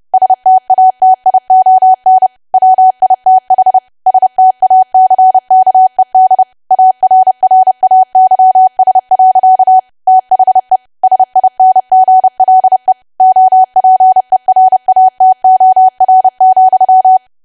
Morse code, heard as a keyed tone, about 30 wpm, 750 Hz. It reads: STATION WITH STACKED ARRAYS. THE SINGLE OPERATOR,
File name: cwa / W1AW-Level3 / 30WPM